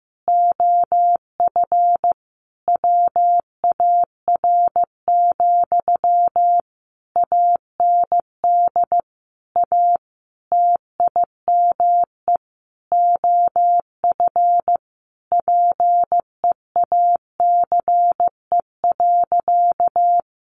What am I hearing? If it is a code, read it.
Morse code, heard as a keyed tone, about 15 wpm, 700 Hz. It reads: OF WAR, AND A TIME OF PEACE.
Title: Sentences from Top 1500 Words 15wpm